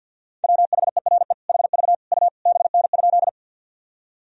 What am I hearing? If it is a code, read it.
Morse code, heard as a keyed tone, about 50 wpm, 700 Hz. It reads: WHERE HV U BN?
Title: Sentences from Top 200 Words and Abbreviations 50wpm